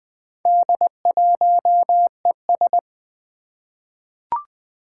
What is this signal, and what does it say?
Morse code, heard as a keyed tone, about 20 wpm, 700 Hz. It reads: D1ES
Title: Four Challenging Letter-Number Combinations 20wpm